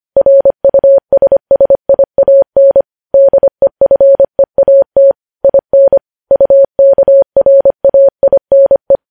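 Morse code, heard as a keyed tone, about 25 wpm, 550 Hz. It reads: RUSSIAN DEFEAT IN UKRAINE
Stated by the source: Morse code news headlines at 25 wpm.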